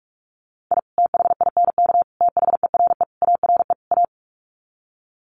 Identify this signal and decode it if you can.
Morse code, heard as a keyed tone, about 44 wpm, 700 Hz. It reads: I THINK THERE ARE A
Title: Sets of 5 Words Encore2 45wpm